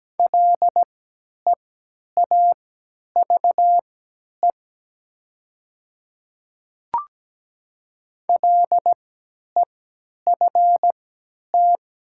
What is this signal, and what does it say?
Morse code, heard as a keyed tone, about 17 wpm, spaced as for 10 wpm, 700 Hz. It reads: LEAVE LEFT
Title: 3x Character Spacing - Top 500 Words - 17wpm